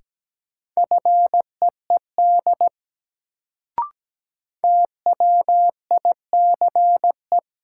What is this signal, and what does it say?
Morse code, heard as a keyed tone, about 17 wpm, 700 Hz. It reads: FEED TWICE
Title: Sentences from Top 1600 Words - New Word Drill - 17wpm